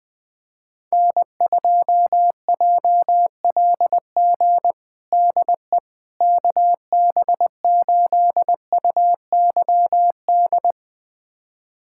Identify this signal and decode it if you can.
Morse code, heard as a keyed tone, about 20 wpm, 700 Hz. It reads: N2JLG DE KB8UYD
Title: USA Callsign DE Callsign 20wpm